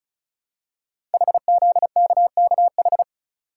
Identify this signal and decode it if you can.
Morse code, heard as a keyed tone, about 36 wpm, 700 Hz. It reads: HZKKH
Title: Code Groups - 5 Letters - 35wpm